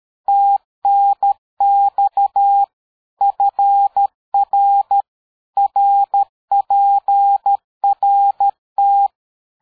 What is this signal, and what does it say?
Morse code, heard as a keyed tone, about 13 wpm, 800 Hz. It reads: TNX FR RPRT
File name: dk5ke / QSO 60BpM